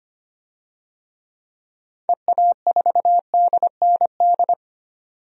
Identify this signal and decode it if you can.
Morse code, heard as a keyed tone, about 25 wpm, 700 Hz. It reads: EA4DND
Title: DXCC-100 Call Signs - 6 Characters - 25wpm